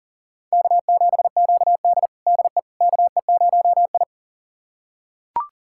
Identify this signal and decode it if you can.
Morse code, heard as a keyed tone, about 40 wpm, 700 Hz. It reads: K7QD DE KE0I
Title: USA Callsign DE Callsign - 4 Characters - 40wpm